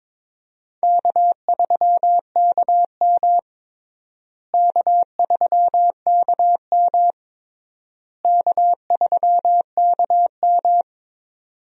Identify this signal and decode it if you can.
Morse code, heard as a keyed tone, about 22 wpm, 700 Hz. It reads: K3KM K3KM K3KM
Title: USA Call Signs - 4 Characters - Repeated 3x - 22wpm